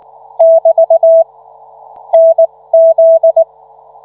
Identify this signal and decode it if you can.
Morse code, heard as a keyed tone, about 19 wpm, spaced as for 16 wpm, 660 Hz. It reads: <BT> NZ